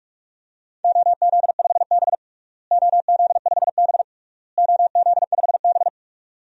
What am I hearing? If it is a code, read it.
Morse code, heard as a keyed tone, about 46 wpm, 700 Hz. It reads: OZ5B OZ5B OZ5B
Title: DXCC-100 Call Signs - 4 Characters - Repeated 3x - 45wpm